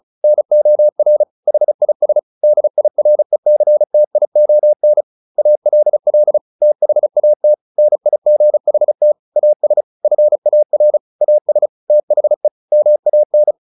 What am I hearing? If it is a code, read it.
Morse code, heard as a keyed tone, about 34 wpm, 600 Hz. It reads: NOR HIS DIRECTION ALL THAT NIGHT AS FAR AS THE MAN
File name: the old man and the sea / the old man and the sea 1 35wpm